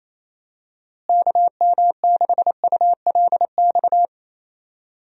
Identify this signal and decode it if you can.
Morse code, heard as a keyed tone, about 28 wpm, 700 Hz. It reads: KM6ULX
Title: USA Call Signs - 6 Characters - Repeated 2x - 28wpm